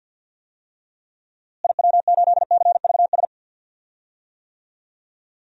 Morse code, heard as a keyed tone, about 51 wpm, 700 Hz. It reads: IW8CVS
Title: DXCC-100 Call Signs - 6 Characters - Repeated 2x - 50wpm